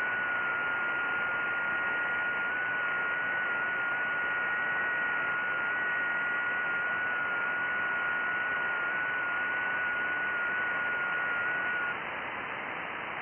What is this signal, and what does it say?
FT8.
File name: IC-7100